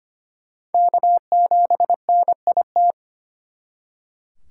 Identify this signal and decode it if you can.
Morse code, heard as a keyed tone, about 31 wpm, 700 Hz. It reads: K7NIT
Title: Speed Racing - USA 1x3 Call Signs 25wpm